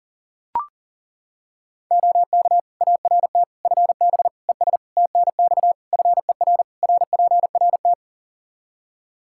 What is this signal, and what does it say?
Morse code, heard as a keyed tone, about 37 wpm, 700 Hz. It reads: OK ART FB ES TNX FER RPRT
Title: Practice QSO - Exchange 2 - Info - 40wpm